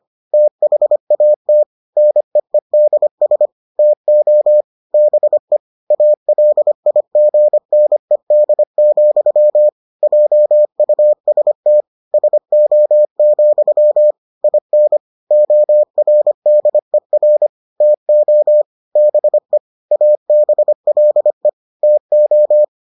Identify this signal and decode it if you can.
Morse code, heard as a keyed tone, about 25 wpm, 600 Hz. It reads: THAT NEEDS TO BE ALIGNED, JUST SO, IN ORDER TO BE ABLE TO